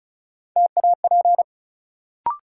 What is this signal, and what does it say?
Morse code, heard as a keyed tone, about 36 wpm, 700 Hz. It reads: TAP
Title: Sentences from Top 1400 Words - New Word Drill - 35wpm